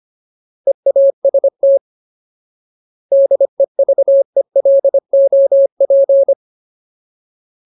Morse code, heard as a keyed tone, about 25 wpm, 550 Hz. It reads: EAST DEVELOP
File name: Ultimate Word List / 25WPM Ultimate